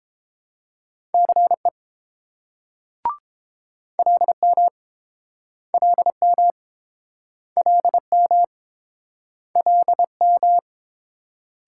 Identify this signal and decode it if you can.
Morse code, heard as a keyed tone, about 32 wpm, 700 Hz. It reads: CE LM LM LM LM